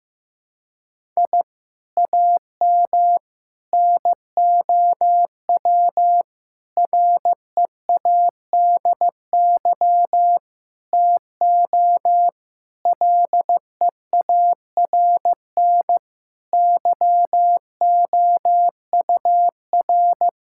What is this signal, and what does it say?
Morse code, heard as a keyed tone, about 15 wpm, 700 Hz. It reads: I AM NOW READY TO LEARN YOUR
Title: Sentences from Top 1000 Words - Review Part 6 - 15wpm